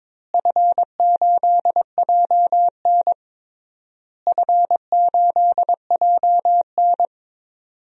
Morse code, heard as a keyed tone, about 22 wpm, 700 Hz. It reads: F8JN F8JN
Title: DXCC-100 Call Signs - 4 Characters - Repeated 3x - 22wpm